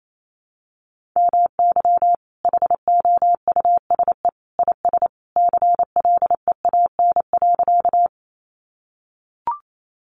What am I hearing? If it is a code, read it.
Morse code, heard as a keyed tone, about 28 wpm, 700 Hz. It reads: MY HOUSE IS CLEAN.
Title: Sentences from Top 1100 Words 28wpm